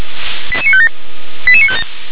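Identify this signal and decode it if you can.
Five-tone selective call (ZVEI), heard as quick runs of five tones, tones of about 70 ms. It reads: 88957
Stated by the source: ZVEI five-tone selective call received on an SDR.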